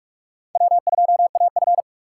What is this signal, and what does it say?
Morse code, heard as a keyed tone, about 46 wpm, 700 Hz. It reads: W2AF
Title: USA Call Signs - 4 Characters - 4x Character Spacing - 45wpm